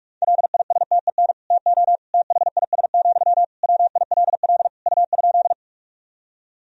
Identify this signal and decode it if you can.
Morse code, heard as a keyed tone, about 46 wpm, 700 Hz. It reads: LISTEN TO THIS, WILL U?
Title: Sentences from Top 400 Words and Abbreviations 45wpm